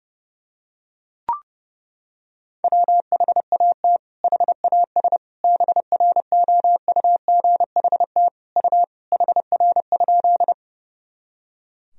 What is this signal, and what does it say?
Morse code, heard as a keyed tone, about 32 wpm, 700 Hz. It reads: WHAT HAS BROUGHT U HR?
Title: Sentences from Top 500 Words and Abbreviations 30wpm